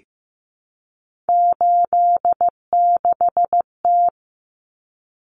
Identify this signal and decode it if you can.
Morse code, heard as a keyed tone, about 15 wpm, 700 Hz. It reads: 86T
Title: Common Cut Numbers - 3-Digit 15wpm